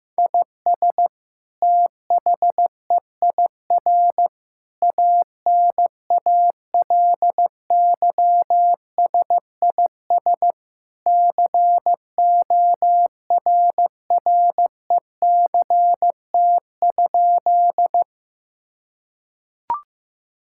Morse code, heard as a keyed tone, about 16 wpm, 700 Hz. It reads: IS THEIR ANALYSIS CORRECT?
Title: Sentences from Top 1000 Words - Review Part 5 - 15wpm